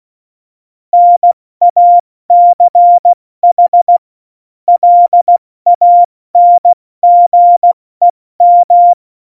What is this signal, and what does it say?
Morse code, heard as a keyed tone, about 16 wpm, 700 Hz. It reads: NACH LANGEM